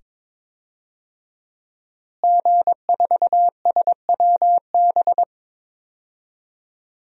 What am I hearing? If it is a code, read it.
Morse code, heard as a keyed tone, about 22 wpm, 700 Hz. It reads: G4SWB
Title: DXCC-100 Call Signs - 5 Characters - Repeated 2x - 22wpm